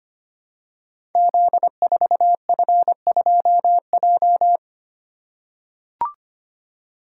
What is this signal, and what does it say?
Morse code, heard as a keyed tone, about 25 wpm, 700 Hz. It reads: Z4F2J
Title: Code Groups - 5 Letter:Numbers - 25wpm